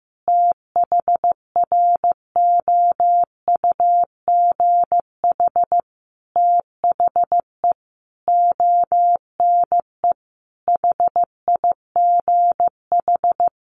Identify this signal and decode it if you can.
Morse code, heard as a keyed tone, about 14 wpm, 700 Hz. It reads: THROUGH THE ONE HIGH
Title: A Sound Of Thunder 15wpm